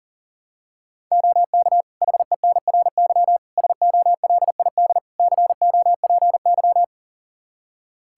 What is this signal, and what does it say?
Morse code, heard as a keyed tone, about 37 wpm, 700 Hz. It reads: OK HENRY SOLID COPY
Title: Practice QSO - Exchange 3 - More Info - 40wpm